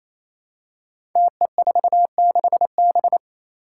Morse code, heard as a keyed tone, about 28 wpm, 700 Hz. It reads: TE46B
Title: Five Challenging Letter-Number Combinations 28wpm